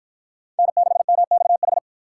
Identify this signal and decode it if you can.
Morse code, heard as a keyed tone, about 52 wpm, 700 Hz. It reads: N6MXH